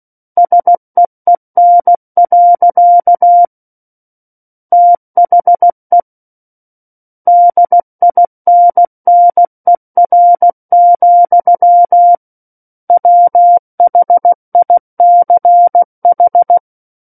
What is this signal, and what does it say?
Morse code, heard as a keyed tone, about 16 wpm, 700 Hz. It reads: SEEN. THE DINNER, WHICH
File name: Book One / 07-Chapter 7 - 16 WPM